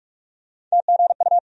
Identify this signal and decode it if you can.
Morse code, heard as a keyed tone, about 45 wpm, 700 Hz. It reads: TGU
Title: USA Call Sign Postfixes - 3 Letters 45wpm